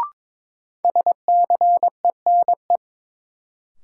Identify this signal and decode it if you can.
Morse code, heard as a keyed tone, about 22 wpm, 700 Hz. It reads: SCENE
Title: Sentences from Top 1100 Words - New Word Drill - 22wpm